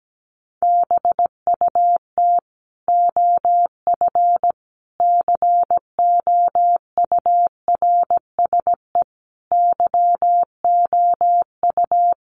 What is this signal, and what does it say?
Morse code, heard as a keyed tone, about 17 wpm, 700 Hz. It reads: BUT OF COURSE YOU
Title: Sentences from Top 1500 Words 17wpm